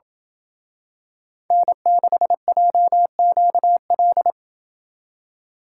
Morse code, heard as a keyed tone, about 27 wpm, 700 Hz. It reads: N6JQL